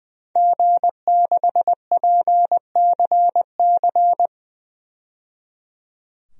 Morse code, heard as a keyed tone, about 21 wpm, 700 Hz. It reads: G6PCC
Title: DXCC-100 Call Signs - 5 Characters - 20wpm